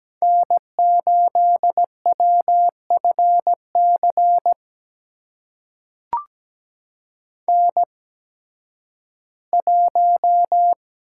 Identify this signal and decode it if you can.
Morse code, heard as a keyed tone, about 17 wpm, spaced as for 11 wpm, 700 Hz. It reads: N8WFC N1
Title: USA Call Signs - 5 Characters - 8x Character Spacing - 17wpm